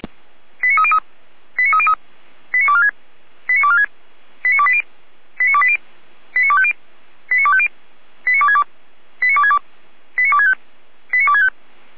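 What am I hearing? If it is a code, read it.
Five-tone selective call (ZVEI), heard as quick runs of five tones, tones of about 70 ms. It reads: 89303 89303 89246 89246 89280 89280 89250 89250 89262 89262 89265 89265
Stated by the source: Five-tone ZVEI selective call 89173 received on SDR, logged by fire brigade Feuerwehr Orlamünde.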